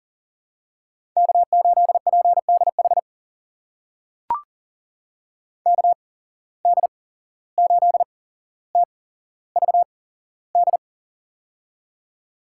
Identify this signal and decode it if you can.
Morse code, heard as a keyed tone, about 38 wpm, spaced as for 16 wpm, 700 Hz. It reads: K8PDH KD8TVD
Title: USA Call Signs - 8x Character Spacing - 40wpm